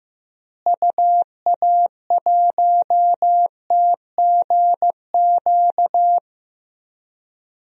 Morse code, heard as a keyed tone, about 15 wpm, 700 Hz. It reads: UA1TGQ
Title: DXCC-100 Call Signs - 6 Characters - Repeated 2x - 15wpm